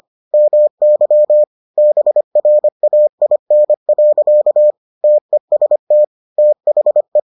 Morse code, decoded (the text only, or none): MY BRAIN. TEST THE